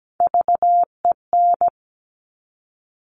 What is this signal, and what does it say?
Morse code, heard as a keyed tone, about 17 wpm, 700 Hz. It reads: VEN
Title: Top Three Letter Combinations in Words 17wpm